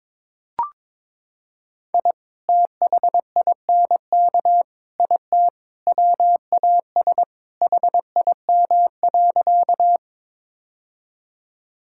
Morse code, heard as a keyed tone, about 22 wpm, 700 Hz. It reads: I THINK IT WAS HIM.
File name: practice / 22wpm / Sentences with 4-Word Phrases - 22wpm